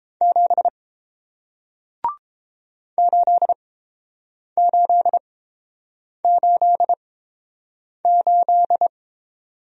Morse code, heard as a keyed tone, about 31 wpm, spaced as for 7 wpm, 700 Hz. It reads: 7 8888